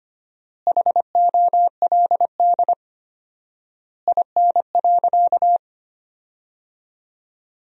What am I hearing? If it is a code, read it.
Morse code, heard as a keyed tone, about 25 wpm, 700 Hz. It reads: HOLD IN.
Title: 4x Word-Spacing - Sentences from Top 400 Words - 25wpm